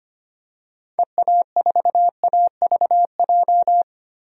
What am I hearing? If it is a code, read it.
Morse code, heard as a keyed tone, about 25 wpm, 700 Hz. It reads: EA4AVJ
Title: DXCC-100 Call Signs - 6 Characters - 25wpm